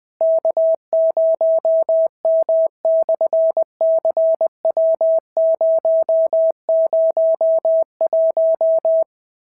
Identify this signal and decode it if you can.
Morse code, heard as a keyed tone, about 20 wpm, 690 Hz. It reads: K0M/CW001
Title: SOTA QSO Practice 20wpm